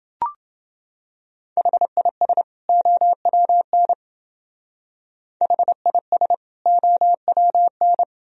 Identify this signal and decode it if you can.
Morse code, heard as a keyed tone, about 28 wpm, 700 Hz. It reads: HIS OWN HIS OWN